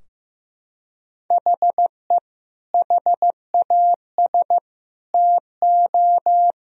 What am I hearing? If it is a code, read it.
Morse code, heard as a keyed tone, about 15 wpm, 700 Hz. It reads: HE HAS TO
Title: Sentences from Top 1000 Words - Review Part 2 - 15wpm